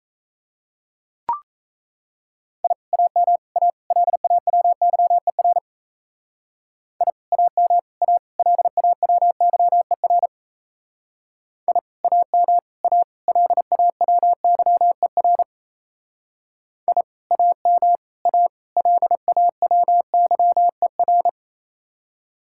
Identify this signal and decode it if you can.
Morse code, heard as a keyed tone, about 33 wpm, 700 Hz. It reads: I AM A LAWYER I AM A LAWYER I AM A LAWYER I AM A LAWYER
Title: Speed-Racing Common QSO Phrases - 28wpm